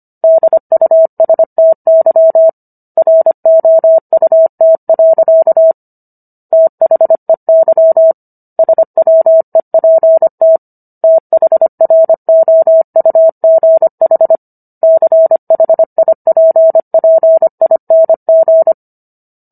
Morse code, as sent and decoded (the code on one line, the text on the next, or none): -.. ..- ... - -.-- / .-. --- ..- - .-.-.- / - .... -.-- / ... .-- . .--. - / - .... .-. --- ..- --. .... / -.-. .... .. .--. .--. .. -. --.
DUSTY ROUT. THEY SWEPT THROUGH CHIPPING